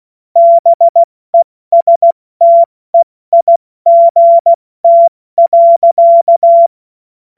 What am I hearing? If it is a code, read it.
Morse code, heard as a keyed tone, about 16 wpm, spaced as for 14 wpm, 680 Hz. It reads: BESTEIGT.
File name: Witz-08-Juli-2025-70-F80